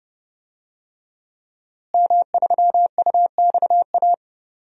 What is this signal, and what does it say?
Morse code, heard as a keyed tone, about 30 wpm, 700 Hz. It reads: M3UXA
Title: DXCC-100 Call Signs - 5 Characters - Repeated 2x - 30wpm